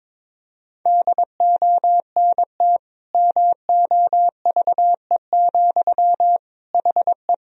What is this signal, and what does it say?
Morse code, heard as a keyed tone, about 21 wpm, 700 Hz. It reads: DONT MOVE, HE
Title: Sentences from Top 300 Words and Abbreviations 22wpm